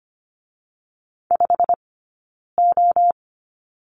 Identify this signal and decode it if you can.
Morse code, heard as a keyed tone, about 24 wpm, 700 Hz. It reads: 5 O
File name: Lesson 15d - Number 5 - Two Characters 25wpm